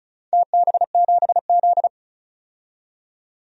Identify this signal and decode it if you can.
Morse code, heard as a keyed tone, about 34 wpm, 700 Hz. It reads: TB7Z
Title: DXCC-100 Call Signs - 4 Characters - 35wpm